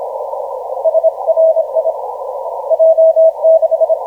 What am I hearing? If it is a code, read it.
Morse code, heard as a keyed tone, about 29 wpm, 650 Hz. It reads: SRI J6
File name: J6 N7QT 1452UTC 10m